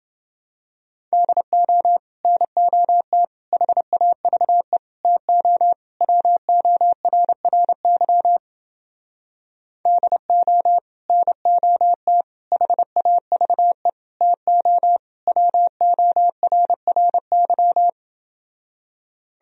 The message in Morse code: DO NOT HAVE TO WORRY DO NOT HAVE TO WORRY

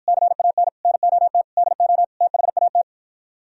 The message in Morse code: CAN NOT DO THAT

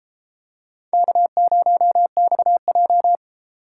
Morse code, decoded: K0XJ

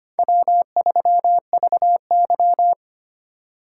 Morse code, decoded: W3VY

